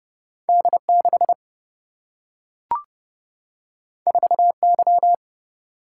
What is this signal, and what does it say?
Morse code, heard as a keyed tone, about 31 wpm, 700 Hz. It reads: D6 4Y
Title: Single-Letter Single-Number 30wpm